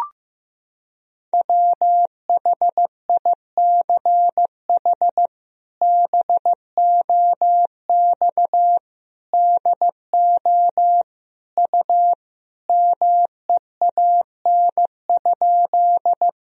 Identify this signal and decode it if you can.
Morse code, heard as a keyed tone, about 15 wpm, 700 Hz. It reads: WHICH BOX DO U MEAN?
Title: Sentences from Top 500 Words and Abbreviations 15wpm